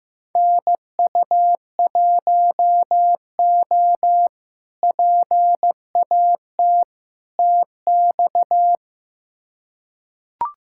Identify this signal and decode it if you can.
Morse code, heard as a keyed tone, about 15 wpm, 700 Hz. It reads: NU1O PAT TX
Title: K1USN SST QSO Practice - 15wpm